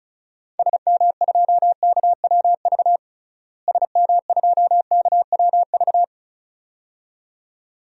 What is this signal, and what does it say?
Morse code, heard as a keyed tone, about 34 wpm, 700 Hz. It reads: SM2KWV SM2KWV
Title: DXCC-100 Call Signs - 6 Characters - Repeated 2x - 35wpm